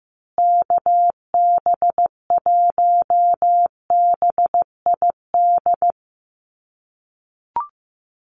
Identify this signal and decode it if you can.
Morse code, heard as a keyed tone, about 15 wpm, 700 Hz. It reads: KB1BID
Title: USA Call Signs Repeated 2x 15wpm